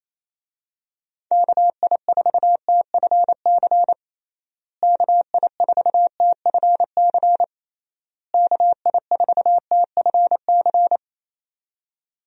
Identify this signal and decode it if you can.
Morse code, heard as a keyed tone, about 27 wpm, 700 Hz. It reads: KI4TFC KI4TFC KI4TFC
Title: USA Call Signs - 6 Characters - Repeated 3x - 28wpm